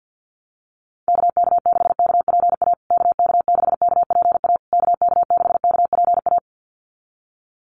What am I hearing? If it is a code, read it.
Morse code, heard as a keyed tone, about 50 wpm, 700 Hz. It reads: KK6KPA KK6KPA KK6KPA
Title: USA Call Signs Repeated 3x 50wpm